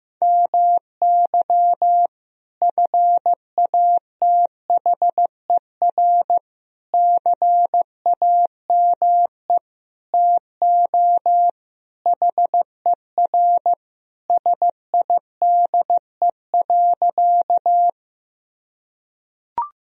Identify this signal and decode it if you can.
Morse code, heard as a keyed tone, about 15 wpm, 700 Hz. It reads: MY FATHER CAME TO HER SIDE.
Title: Sentences from Top 1000 Words - Review Part 1 - 15wpm